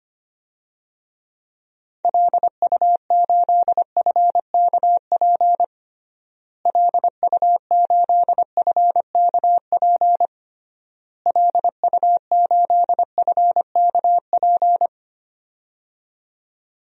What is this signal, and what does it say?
Morse code, heard as a keyed tone, about 25 wpm, 700 Hz. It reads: LU8FKP LU8FKP LU8FKP
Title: DXCC-100 Call Signs - 6 Characters - Repeated 3x - 25wpm